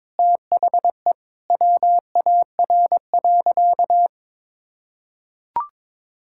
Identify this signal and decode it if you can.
Morse code, heard as a keyed tone, about 22 wpm, 700 Hz. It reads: THE WAR.
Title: Sentences from Top 1000 Words - Review Part 6 - 22wpm